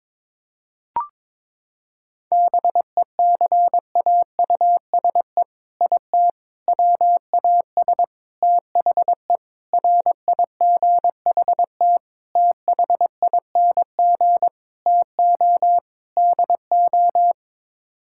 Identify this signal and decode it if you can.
Morse code, heard as a keyed tone, about 22 wpm, 700 Hz. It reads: BECAUSE IT WAS THE RIGHT THING TO DO
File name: practice / 22wpm / Sets of 8 Words - 22wpm